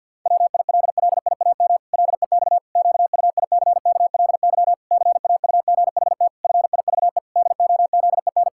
Morse code, heard as a keyed tone, about 48 wpm, 700 Hz. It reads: WILLIAM LEX QUICKLY CAUGHT FIVE DOZEN
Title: Pangram Challenge v2 - 50wpm